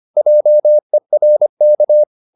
Morse code, heard as a keyed tone, about 25 wpm, 600 Hz. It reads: JERK
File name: the old man and the sea part 1 25wpm